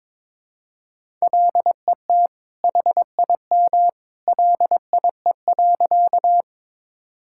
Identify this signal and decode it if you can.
Morse code, heard as a keyed tone, about 22 wpm, 700 Hz. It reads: LET HIM LIE.
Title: Sentences from Top 1600 Words 22wpm